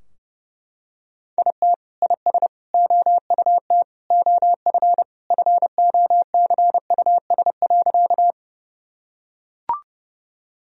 Morse code, heard as a keyed tone, about 32 wpm, 700 Hz. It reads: IT IS OUT OF FOCUS.
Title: Sentences from Top 1000 Words - Review Part 4 - 30wpm